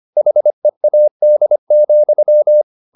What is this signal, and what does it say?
Morse code, heard as a keyed tone, about 25 wpm, 600 Hz. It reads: HEAD,